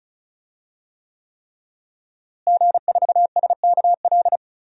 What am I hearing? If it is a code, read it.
Morse code, heard as a keyed tone, about 36 wpm, 700 Hz. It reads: G4SKL
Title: DXCC-100 Call Signs - 5 Characters - Repeated 3x - 35wpm